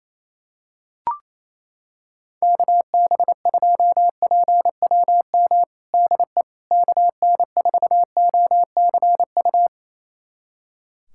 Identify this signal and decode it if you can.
Morse code, heard as a keyed tone, about 28 wpm, 700 Hz. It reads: KB2PWM DE KN4OCU
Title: USA Callsign DE Callsign - 6 Characters - 28wpm